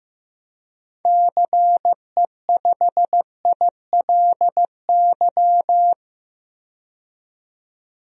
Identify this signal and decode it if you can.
Morse code, heard as a keyed tone, about 14 wpm, 700 Hz. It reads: CE5ILY
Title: DXCC-100 Call Signs - 6 Characters - 15wpm